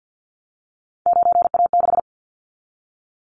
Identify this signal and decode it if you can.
Morse code, heard as a keyed tone, about 50 wpm, 700 Hz. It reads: 9A6
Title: Callsign Prefixes for Top 50 DXCC - 50wpm